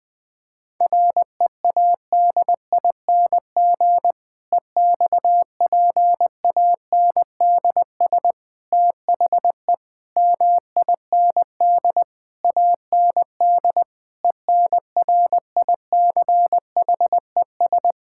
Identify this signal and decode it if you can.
Morse code, heard as a keyed tone, about 20 wpm, 700 Hz. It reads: READING EXPANDS THE MIND AND ENRICHES
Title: Difficult Sentences with Hints - 20wpm